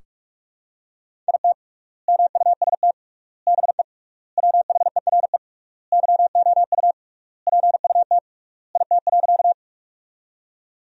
Morse code, heard as a keyed tone, about 46 wpm, 700 Hz. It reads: IT MUST BE WHERE YOU PUT IT.
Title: 3x Word-Spacing - Sentences from Top 200 Words - 45wpm